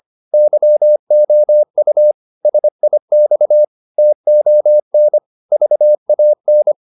Morse code, heard as a keyed tone, about 25 wpm, 600 Hz. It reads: YOU SIX TON VAN